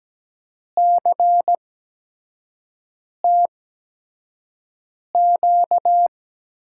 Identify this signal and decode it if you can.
Morse code, heard as a keyed tone, about 17 wpm, spaced as for 5 wpm, 700 Hz. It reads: CTQ
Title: USA Call Signs - 5 Characters - 8x Character Spacing - 17wpm